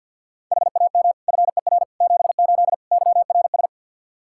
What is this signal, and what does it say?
Morse code, heard as a keyed tone, about 45 wpm, 700 Hz. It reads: HAM FER 78 YRS
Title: Practice QSO - Exchange 3 - More Info - 50wpm